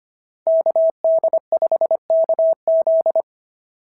Morse code, heard as a keyed tone, about 25 wpm, 690 Hz. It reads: KD5KZ
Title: SOTA QSO Practice 25wpm